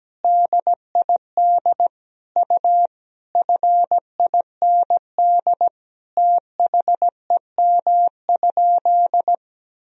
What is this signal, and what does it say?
Morse code, heard as a keyed tone, about 17 wpm, 700 Hz. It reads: DID U FIND THEM?
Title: Sentences from Top 200 Words and Abbreviations 17wpm